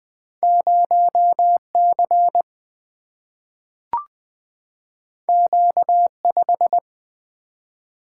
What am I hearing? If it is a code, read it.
Morse code, heard as a keyed tone, about 20 wpm, 700 Hz. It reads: 0 C Q 5
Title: Single-Letter Single-Number 20wpm